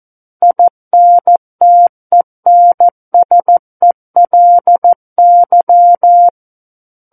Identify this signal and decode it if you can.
Morse code, heard as a keyed tone, about 14 wpm, 700 Hz. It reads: INTENSELY
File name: Book One / 05-Chapter 5 - 14 WPM